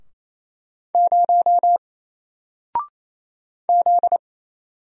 Morse code, heard as a keyed tone, about 28 wpm, 700 Hz. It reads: 0 Z
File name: Single-Letter-Number-28wpm